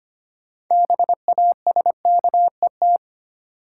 Morse code, heard as a keyed tone, about 25 wpm, 700 Hz. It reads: BASKET
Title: Six Letter Words - 25wpm